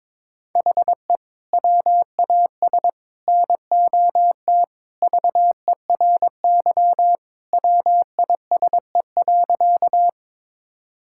Morse code, heard as a keyed tone, about 22 wpm, 700 Hz. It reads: HE WAS NOT VERY WISE.
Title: Sentences from Top 1600 Words 22wpm